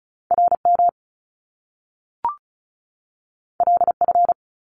Morse code, heard as a keyed tone, about 36 wpm, 700 Hz. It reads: RM LF
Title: Top Two Letter Combinations in Words 35wpm